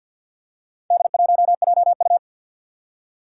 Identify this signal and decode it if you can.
Morse code, heard as a keyed tone, about 46 wpm, 700 Hz. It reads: D1JU
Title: Four Challenging Letter-Number Combinations 50wpm